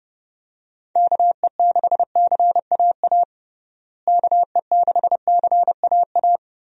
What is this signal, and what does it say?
Morse code, heard as a keyed tone, about 30 wpm, 700 Hz. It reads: KE6CAA KE6CAA
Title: USA Call Signs - 6 Characters - Repeated 3x - 30wpm